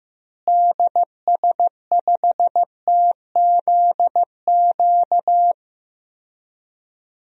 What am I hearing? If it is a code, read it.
Morse code, heard as a keyed tone, about 15 wpm, 700 Hz. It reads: DS5TZQ
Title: DXCC-100 Call Signs - 6 Characters - Repeated 3x - 15wpm